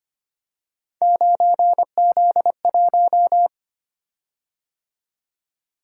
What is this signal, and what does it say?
Morse code, heard as a keyed tone, about 25 wpm, 700 Hz. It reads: 9Z1
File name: Three Letter-Number Combinations - 25wpm